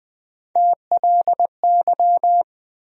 Morse code, heard as a keyed tone, about 20 wpm, 700 Hz. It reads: TLY
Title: USA Call Sign Postfixes - 3 Letters 20wpm